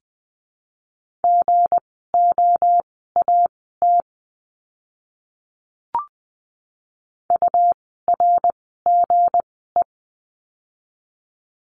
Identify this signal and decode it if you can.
Morse code, heard as a keyed tone, about 20 wpm, spaced as for 14 wpm, 700 Hz. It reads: GOAT URGE
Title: 160 Lesson 26f - Letter g - Words 20wpm